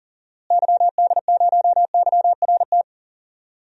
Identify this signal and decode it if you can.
Morse code, heard as a keyed tone, about 40 wpm, 700 Hz. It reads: YD0YRT
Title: DXCC-100 Call Signs - 6 Characters - Repeated 2x - 40wpm